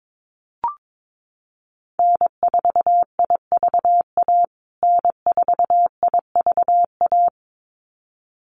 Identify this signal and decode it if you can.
Morse code, heard as a keyed tone, about 22 wpm, 700 Hz. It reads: N4IVA N4IVA
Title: USA Call Signs Repeated 2x 22wpm